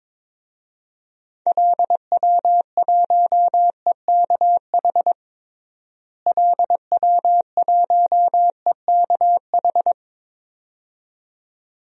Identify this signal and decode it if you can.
Morse code, heard as a keyed tone, about 22 wpm, 700 Hz. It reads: LW1EKH LW1EKH
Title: DXCC-100 Call Signs - 6 Characters - Repeated 2x - 22wpm